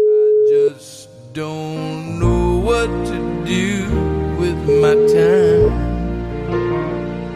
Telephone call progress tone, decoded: ringback tone

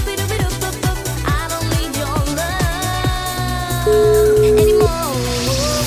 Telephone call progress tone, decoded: ringback tone

